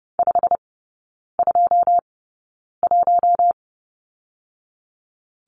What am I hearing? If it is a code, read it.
Morse code, heard as a keyed tone, about 30 wpm, spaced as for 9 wpm, 700 Hz. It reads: 521
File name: course / 30:10wpm / 087 Lesson 15e - Number 5 - Three Characters 30wpm